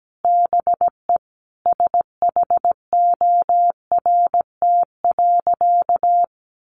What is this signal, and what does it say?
Morse code, heard as a keyed tone, about 17 wpm, 700 Hz. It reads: BE SHORT.
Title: Sentences from Top 1400 Words 17wpm